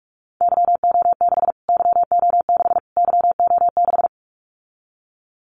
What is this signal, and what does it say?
Morse code, heard as a keyed tone, about 48 wpm, 700 Hz. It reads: YO6 YO6 YO6
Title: Callsign Prefixes for Top 25 DXCC - Repeated 3x - 45wpm